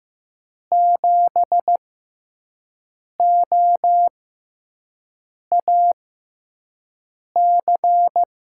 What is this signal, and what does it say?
Morse code, heard as a keyed tone, about 15 wpm, spaced as for 5 wpm, 700 Hz. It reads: 7OAC
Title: USA Call Signs - 5 Characters - 6x Character Spacing - 15wpm